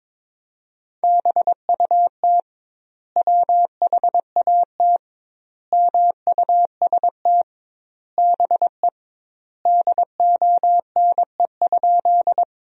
Morse code, heard as a keyed tone, about 22 wpm, 700 Hz. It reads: BUT WHAT MUST BE DONE?
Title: 2x Word-Spacing - Sentences from Top 500 Words - 22wpm